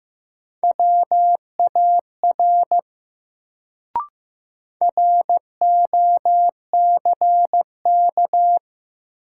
Morse code, heard as a keyed tone, about 15 wpm, 700 Hz. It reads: WAR ROCK
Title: Sentences from Top 400 Words - New Word Drill - 15wpm